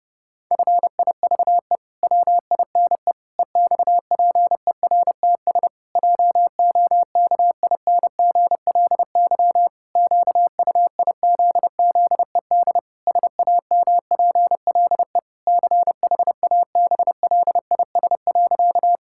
Morse code, heard as a keyed tone, about 32 wpm, 700 Hz. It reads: FIVE WINE EXPERTS JOKINGLY QUIZZED SAMPLE CHABLIS.
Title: Pangram Challenge v2 - 30wpm